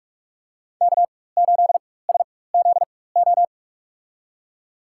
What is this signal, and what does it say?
Morse code, heard as a keyed tone, about 43 wpm, spaced as for 21 wpm, 700 Hz. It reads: K8SZO
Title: USA Call Signs - 5 Characters - 4x Character Spacing - 45wpm